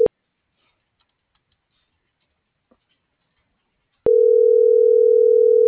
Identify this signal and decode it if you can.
Ringback tone (North America), 440 + 480 Hz.